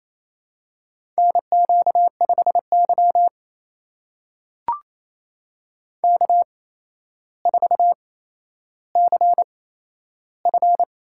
Morse code, heard as a keyed tone, about 27 wpm, spaced as for 12 wpm, 700 Hz. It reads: NQ5Y K4CF
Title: USA Call Signs - 4 Characters - 8x Character Spacing - 28wpm